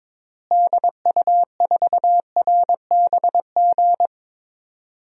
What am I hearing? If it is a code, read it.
Morse code, heard as a keyed tone, about 22 wpm, 700 Hz. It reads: DU4RBG
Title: DXCC-100 Call Signs - 6 Characters - Repeated 3x - 22wpm